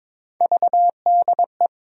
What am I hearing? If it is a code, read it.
Morse code, heard as a keyed tone, about 21 wpm, 700 Hz. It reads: VDE
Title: USA Call Sign Postfixes - 3 Letters 22wpm